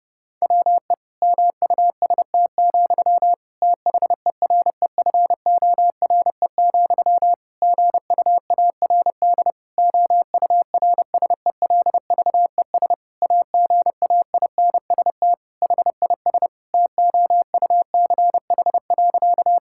Morse code, heard as a keyed tone, about 30 wpm, 700 Hz. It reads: WE MUST, THEREFORE, GUARD OURSELVES AGAINST HIS TOUCH.
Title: Sentences from Top 1600 Words 30wpm